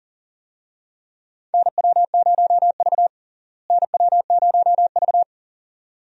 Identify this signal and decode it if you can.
Morse code, heard as a keyed tone, about 37 wpm, 700 Hz. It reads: NW0V NW0V
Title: USA Call Signs - 4 Characters - Repeated 2x - 40wpm